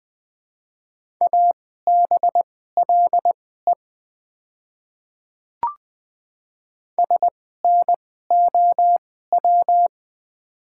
Morse code, heard as a keyed tone, about 20 wpm, spaced as for 14 wpm, 690 Hz. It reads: ABLE SNOW
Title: 2x Character Spacing - Top 500 Words - 20wpm